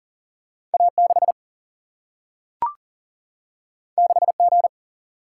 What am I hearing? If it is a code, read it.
Morse code, heard as a keyed tone, about 38 wpm, 700 Hz. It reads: A 6 6 G
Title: Single-Letter Single-Number 40wpm